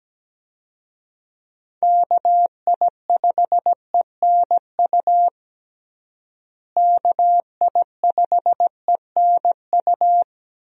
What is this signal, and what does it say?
Morse code, heard as a keyed tone, about 17 wpm, 700 Hz. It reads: KI5ENU KI5ENU
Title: USA Call Signs - 6 Characters - Repeated 2x - 17wpm